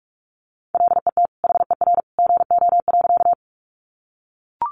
Morse code, heard as a keyed tone, about 44 wpm, 700 Hz. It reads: LET HER GO.
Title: Sentences from Top 300 Words 45wpm